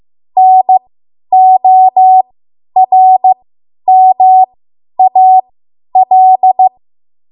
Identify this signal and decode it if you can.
Morse code, heard as a keyed tone, about 15 wpm, spaced as for 10 wpm, 750 Hz. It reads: NORMAL
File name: Morse / 10WPM